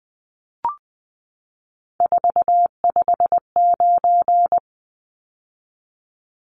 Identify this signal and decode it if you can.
Morse code, heard as a keyed tone, about 20 wpm, 700 Hz. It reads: 459
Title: Numbers - 3-Digit 20wpm